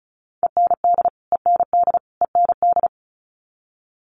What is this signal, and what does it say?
Morse code, heard as a keyed tone, about 34 wpm, 700 Hz. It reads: END END END
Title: Top 200 Words Repeated 35wpm